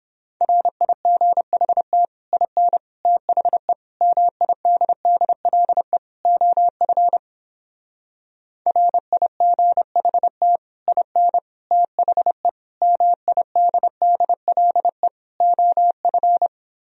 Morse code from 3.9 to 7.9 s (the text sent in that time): MIDDLE OF